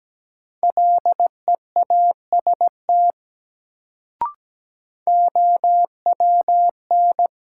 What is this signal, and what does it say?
Morse code, heard as a keyed tone, about 17 wpm, 700 Hz. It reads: LEAST OWN
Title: Sets of 2 Words - New Word Drill - 17wpm